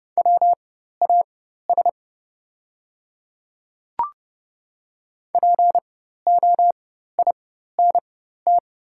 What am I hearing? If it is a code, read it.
Morse code, heard as a keyed tone, about 31 wpm, spaced as for 14 wpm, 700 Hz. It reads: WAS POINT
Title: 4x Character Spacing - Top 200 Words - 30wpm